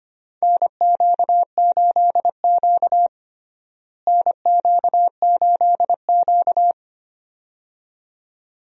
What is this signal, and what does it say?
Morse code, heard as a keyed tone, about 25 wpm, 700 Hz. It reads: NQ8Q NQ8Q
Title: USA Call Signs - 4 Characters - Repeated 2x - 25wpm